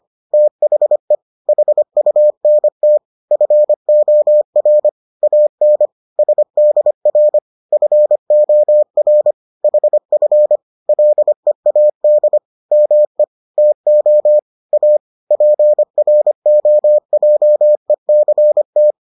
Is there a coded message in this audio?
Yes: Morse code from 0.3 to 18.9 s, heard as a keyed tone, about 25 wpm, 600 Hz. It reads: THE HUNT FOR AN SDR FOR HF LEAD ME TO A PROJECT